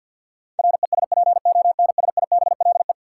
Morse code, heard as a keyed tone, about 52 wpm, 700 Hz. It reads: RESPONSIBLE
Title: Sentences from Top 1000 Words - Review Part 7 - New Word Drill - 50wpm